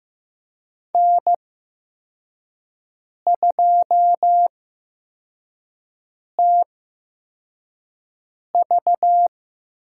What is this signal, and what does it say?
Morse code, heard as a keyed tone, about 15 wpm, spaced as for 4 wpm, 700 Hz. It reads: N2TV
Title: USA Call Signs - 8x Character Spacing - 15wpm